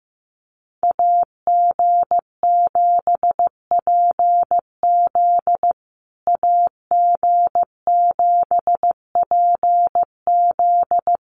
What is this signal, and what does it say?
Morse code, heard as a keyed tone, about 14 wpm, 700 Hz. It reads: AG7PZ AG7PZ
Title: USA Call Signs Repeated 2x 15wpm